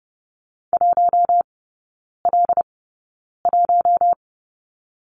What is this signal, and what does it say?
Morse code, heard as a keyed tone, about 30 wpm, 700 Hz. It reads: 1 L 1
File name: Lesson 19e - Letter w - Three Characters 30wpm